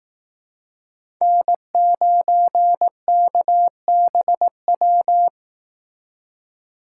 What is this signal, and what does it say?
Morse code, heard as a keyed tone, about 18 wpm, 700 Hz. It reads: N9KBW
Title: Speed Racing - USA 1x3 Call Signs 15wpm